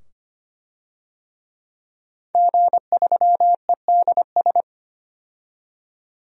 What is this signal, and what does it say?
Morse code, heard as a keyed tone, about 25 wpm, 700 Hz. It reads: G3EDS
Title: DXCC-100 Call Signs - 5 Characters - 25wpm